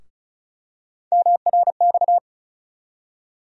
Morse code, heard as a keyed tone, about 34 wpm, 700 Hz. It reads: MRX
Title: USA Call Sign Postfixes - 3 Letters 35wpm